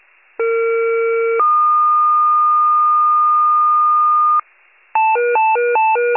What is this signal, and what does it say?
Two-tone page, heard as a one-second tone, then a three-second tone, about 460 then 1220 Hz. It is heard from 0.4 to 4.4 s.